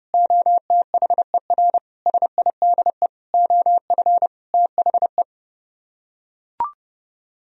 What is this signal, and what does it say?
Morse code, heard as a keyed tone, about 30 wpm, 700 Hz. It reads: OTHER SIDE OF THE